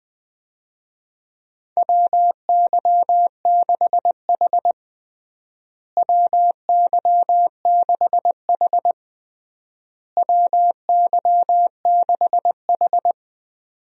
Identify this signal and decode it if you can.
Morse code, heard as a keyed tone, about 20 wpm, 700 Hz. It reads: WY6H WY6H WY6H
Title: USA Call Signs - 4 Characters - Repeated 3x - 20wpm